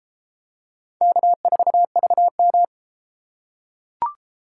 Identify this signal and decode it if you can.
Morse code, heard as a keyed tone, about 32 wpm, 700 Hz. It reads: K4VM